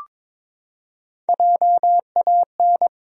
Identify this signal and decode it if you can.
Morse code, heard as a keyed tone, about 22 wpm, 700 Hz. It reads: JAN
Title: USA Common Names - Max 5 Letters - 22wpm